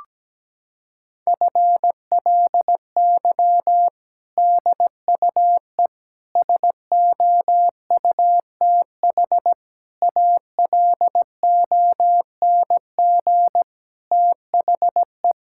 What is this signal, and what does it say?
Morse code, heard as a keyed tone, about 17 wpm, 700 Hz. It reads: FLY DUE SOUTH ALONG THE
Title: Sentences from Top 1000 Words - Review Part 6 - 17wpm